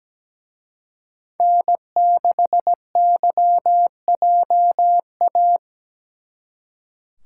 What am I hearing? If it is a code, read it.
Morse code, heard as a keyed tone, about 20 wpm, 700 Hz. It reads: N6YJA
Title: Speed Racing - USA 1x3 Call Signs 17wpm